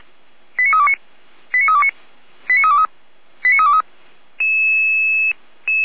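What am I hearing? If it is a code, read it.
Five-tone selective call (ZVEI), heard as quick runs of five tones, tones of about 70 ms. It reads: 89329 89329 89323 89323
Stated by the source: ZVEI five-tone selective call received by SDR.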